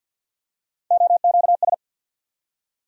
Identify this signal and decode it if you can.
Morse code, heard as a keyed tone, about 53 wpm, 690 Hz. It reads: OQS
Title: USA Call Sign Postfixes - 3 Letters 50wpm